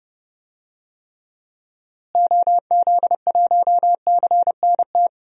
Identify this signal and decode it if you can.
Morse code, heard as a keyed tone, about 30 wpm, 700 Hz. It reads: OZ1CNT
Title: DXCC-100 Call Signs - 6 Characters - Repeated 2x - 30wpm